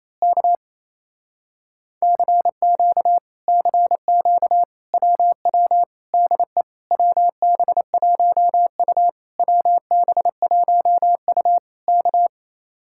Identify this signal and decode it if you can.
Morse code, heard as a keyed tone, about 26 wpm, 700 Hz. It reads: K CQ CQ WW DE WB1U WB1U K
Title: Speed-Racing Common QSO Phrases - 28wpm